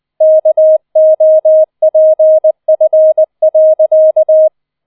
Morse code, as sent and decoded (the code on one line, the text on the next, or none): -.- --- .--. ..-. .-.-.-
KOPF.